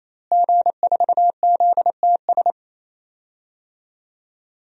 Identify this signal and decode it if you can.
Morse code, heard as a keyed tone, about 27 wpm, 700 Hz. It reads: G4ZTS
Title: DXCC-100 Call Signs - 5 Characters - Repeated 3x - 28wpm